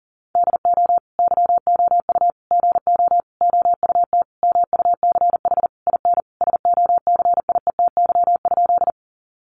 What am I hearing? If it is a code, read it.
Morse code, heard as a keyed tone, about 40 wpm, 700 Hz. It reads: DO YOU GO OUT MUCH IN SOCIETY?
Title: Sentences from Top 700 Words 40wpm